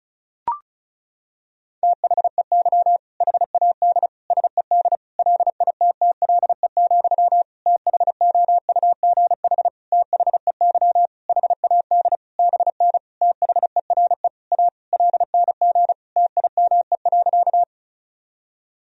Morse code, heard as a keyed tone, about 36 wpm, 700 Hz. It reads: THEY HAD SED LITTLE, THOUGH THEY HAD BN THERE A LNG TIME.
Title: Sentences from Top 400 Words and Abbreviations 35wpm